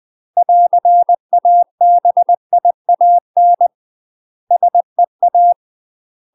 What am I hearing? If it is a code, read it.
Morse code, heard as a keyed tone, about 20 wpm, 700 Hz. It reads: <AR>ABIAN SEA
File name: UK MASTER QSOs TRAINING / 20WPM / UK MASTER QSOs 0032